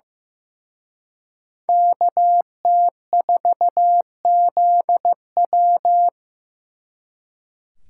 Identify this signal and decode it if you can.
Morse code, heard as a keyed tone, about 18 wpm, 700 Hz. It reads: KT4ZW
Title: Speed Racing - USA 2x2 Call Signs 15wpm